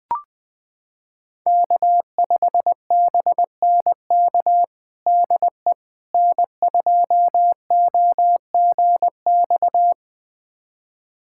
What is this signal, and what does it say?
Morse code, heard as a keyed tone, about 20 wpm, 700 Hz. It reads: K5BNK DE N2OGX
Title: USA Callsign DE Callsign - 5 Characters - 20wpm